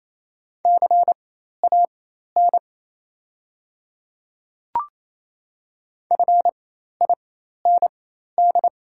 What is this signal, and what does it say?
Morse code, heard as a keyed tone, about 28 wpm, spaced as for 13 wpm, 690 Hz. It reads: CAN FIND
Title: 4x Character Spacing - Top 100 Words - 28wpm